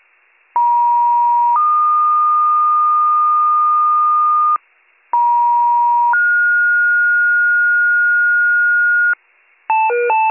Two-tone page, heard as a one-second tone, then a three-second tone, about 950 then 1220 Hz, about 950 then 1430 Hz, with noise. It is heard from 0.5 to 9.1 s.